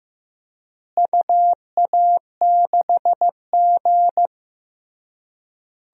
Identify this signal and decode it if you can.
Morse code, heard as a keyed tone, about 16 wpm, 700 Hz. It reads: UA6G
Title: DXCC-100 Call Signs - 4 Characters - Repeated 3x - 15wpm